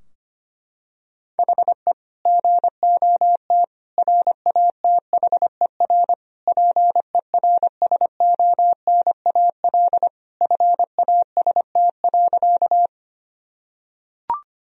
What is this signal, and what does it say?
Morse code, heard as a keyed tone, about 25 wpm, 700 Hz. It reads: HE GOT RATHER PERSONAL FAST.
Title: Sentences from Top 1000 Words - Review Part 4 - 25wpm